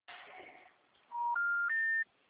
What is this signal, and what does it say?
Special information tone.